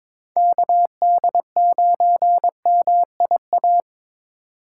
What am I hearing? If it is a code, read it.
Morse code, heard as a keyed tone, about 22 wpm, 700 Hz. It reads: KD9MIA
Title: USA Call Signs - 4x Character Spacing - 22wpm